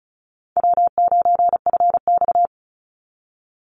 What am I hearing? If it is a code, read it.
Morse code, heard as a keyed tone, about 34 wpm, 700 Hz. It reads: W9FX
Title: Easy Call Signs Repeated 2x - 35wpm